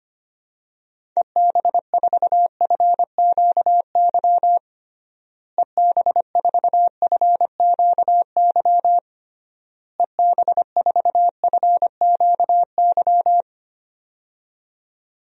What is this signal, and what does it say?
Morse code, heard as a keyed tone, about 25 wpm, 700 Hz. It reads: EB4FQY EB4FQY EB4FQY
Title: DXCC-100 Call Signs - 6 Characters - Repeated 3x - 25wpm